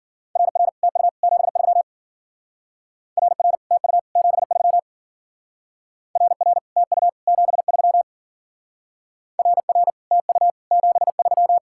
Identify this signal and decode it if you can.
Morse code, heard as a keyed tone, about 51 wpm, 700 Hz. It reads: RR TU 73 RR TU 73 RR TU 73 RR TU 73